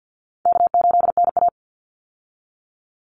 Morse code, heard as a keyed tone, about 51 wpm, 700 Hz. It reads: K8NA
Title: Easy Call Signs - 50wpm